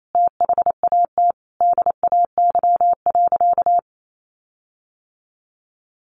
Morse code, heard as a keyed tone, about 28 wpm, 700 Hz. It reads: THAT DAY.
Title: Sentences from Top 1500 Words 28wpm